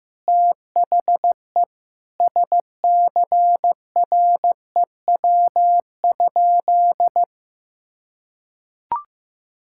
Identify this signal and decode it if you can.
Morse code, heard as a keyed tone, about 16 wpm, 700 Hz. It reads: THE SCREW?
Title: Sentences from Top 1600 Words 15wpm